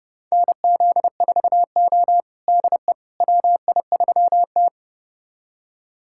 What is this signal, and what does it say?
Morse code, heard as a keyed tone, about 30 wpm, 700 Hz. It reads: NZ4O DE WI3T
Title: USA Callsign DE Callsign - 4 Characters - 30wpm